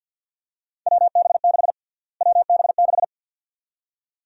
Morse code, heard as a keyed tone, about 51 wpm, 700 Hz. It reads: WB6 WB6
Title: USA Call Sign Prefixes - Repeated 2x - 50wpm